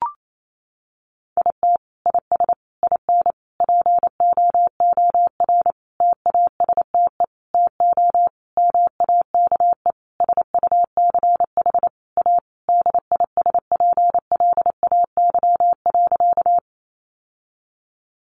Morse code, heard as a keyed tone, about 28 wpm, 700 Hz. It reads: IT IS IN POOR TASTE TO MAKE SUCH A DISPLAY.
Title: Sentences from Top 1200 Words 28wpm